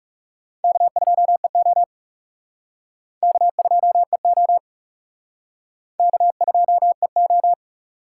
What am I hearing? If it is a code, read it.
Morse code, heard as a keyed tone, about 36 wpm, 700 Hz. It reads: K2EO K2EO K2EO
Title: Speed Racing - USA 1x2 Call Signs 30wpm